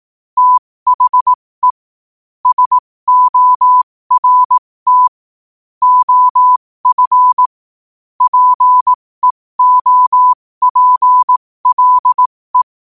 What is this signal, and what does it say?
Morse code, heard as a keyed tone, about 18 wpm, spaced as for 15 wpm, 1000 Hz. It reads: THE SORT OF PEOPLE
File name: qotdfiles / 15wpm